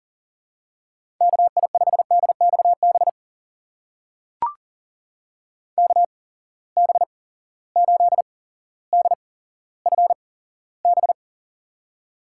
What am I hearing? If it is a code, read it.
Morse code, heard as a keyed tone, about 39 wpm, spaced as for 17 wpm, 700 Hz. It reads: KI5DXB KB8DFB
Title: USA Call Signs - 6 Characters - 8x Character Spacing - 40wpm